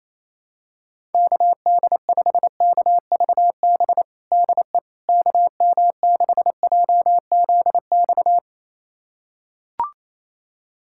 Morse code, heard as a keyed tone, about 27 wpm, 700 Hz. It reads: KD5KVB DE KM6JZX
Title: USA Callsign DE Callsign - 6 Characters - 28wpm